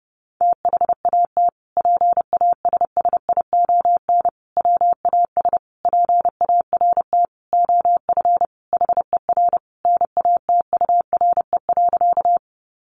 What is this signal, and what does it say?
Morse code, heard as a keyed tone, about 30 wpm, 700 Hz. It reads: THAT PASSION WAS PART OF HER NATURE.
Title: Sentences from Top 1300 Words 30wpm